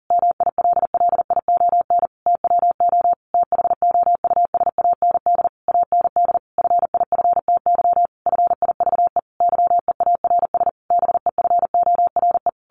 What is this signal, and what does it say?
Morse code, heard as a keyed tone, about 40 wpm, 700 Hz. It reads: MILLION TWO THOUSAND AND FIFTY FIVE YEARS BEFORE
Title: A Sound Of Thunder 40wpm